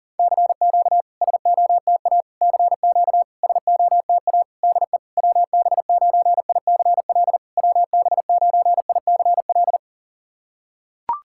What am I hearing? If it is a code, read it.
Morse code, heard as a keyed tone, about 41 wpm, 690 Hz. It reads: CQ SOTA CQ SOTA DE WB9ICL WB9ICL
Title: SOTA QSO Practice 40wpm